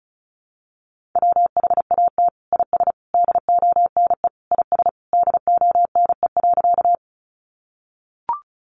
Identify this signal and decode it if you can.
Morse code, heard as a keyed tone, about 34 wpm, 700 Hz. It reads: WHAT IS DONE IS DONE.
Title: Sentences from Top 500 Words 35wpm